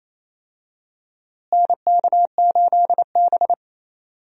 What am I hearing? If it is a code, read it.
Morse code, heard as a keyed tone, about 28 wpm, 700 Hz. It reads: NK8B
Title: USA Call Signs - 4 Characters - Repeated 2x - 28wpm